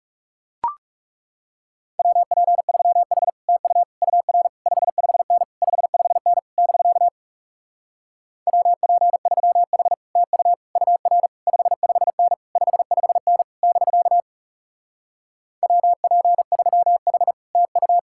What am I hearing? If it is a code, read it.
Morse code, heard as a keyed tone, about 40 wpm, 700 Hz. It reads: WP3H TU UR 55N 55N <BK> WP3H TU UR 55N 55N <BK> WP3H TU